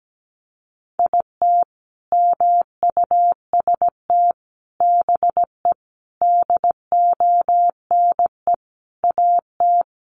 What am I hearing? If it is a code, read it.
Morse code, heard as a keyed tone, about 17 wpm, 700 Hz. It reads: IT MUST BE DONE AT
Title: Sentences from Top 500 Words 17wpm